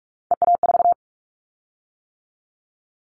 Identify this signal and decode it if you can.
Morse code, heard as a keyed tone, about 46 wpm, 700 Hz. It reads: EA4
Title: Callsign Prefixes for Top 25 DXCC - 45wpm